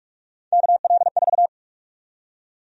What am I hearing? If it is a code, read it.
Morse code, heard as a keyed tone, about 46 wpm, 700 Hz. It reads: KL4
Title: USA Call Sign Prefixes - Repeated 2x - 45wpm